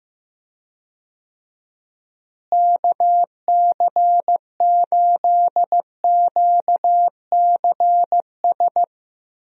Morse code, heard as a keyed tone, about 15 wpm, 700 Hz. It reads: KC8QCS
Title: USA Call Signs - 6 Characters - Repeated 3x - 15wpm